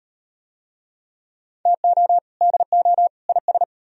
Morse code, heard as a keyed tone, about 36 wpm, 700 Hz. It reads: TO DO IS